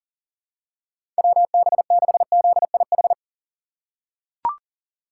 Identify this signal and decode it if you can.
Morse code, heard as a keyed tone, about 40 wpm, 700 Hz. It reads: WB6ZIH
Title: USA Call Signs - 6 Characters - 3x Character Spacing - 40wpm